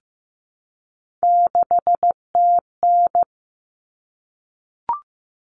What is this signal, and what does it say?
Morse code, heard as a keyed tone, about 15 wpm, 700 Hz. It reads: 6TN
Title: Common Cut Numbers - 3-Digit 15wpm